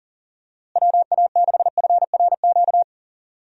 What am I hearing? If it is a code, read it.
Morse code, heard as a keyed tone, about 40 wpm, 700 Hz. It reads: WA6FRQ
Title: USA Call Signs - 6 Characters - Repeated 2x - 40wpm